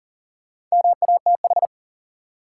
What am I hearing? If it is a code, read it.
Morse code, heard as a keyed tone, about 40 wpm, 700 Hz. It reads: MATH
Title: Sentences from Top 1000 Words - Review Part 6 - New Word Drill - 40wpm